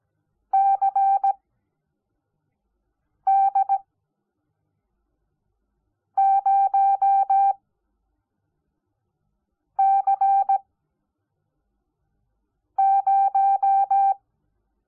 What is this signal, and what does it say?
Morse code, heard as a keyed tone, about 17 wpm, spaced as for 4 wpm, 780 Hz. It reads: CD0C0
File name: Onlinelehrgang / morsen / DJ4UF Morsen05